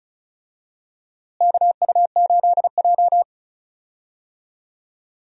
Morse code, heard as a keyed tone, about 34 wpm, 700 Hz. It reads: KU8J
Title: USA Call Signs - 4 Characters - Repeated 2x - 35wpm